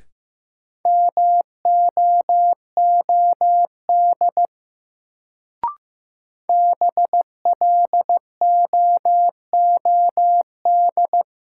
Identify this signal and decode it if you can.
Morse code, heard as a keyed tone, about 15 wpm, 700 Hz. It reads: MOOD BLOOD
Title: Sentences from Top 1100 Words - New Word Drill Alt - 15wpm